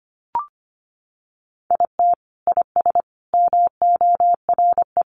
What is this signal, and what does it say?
Morse code, heard as a keyed tone, about 25 wpm, 700 Hz. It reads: IT IS MORE
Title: Sentences from Top 600 Words 25wpm